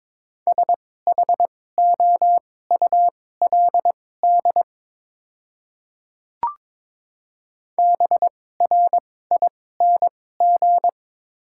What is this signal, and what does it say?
Morse code, heard as a keyed tone, about 21 wpm, spaced as for 16 wpm, 700 Hz. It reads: SHOULD BRING
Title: 2x Character Spacing - Top 500 Words - 22wpm